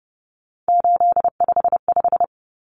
Morse code, heard as a keyed tone, about 30 wpm, 700 Hz. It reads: 855
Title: Common Cut Numbers - 3-Digit 30wpm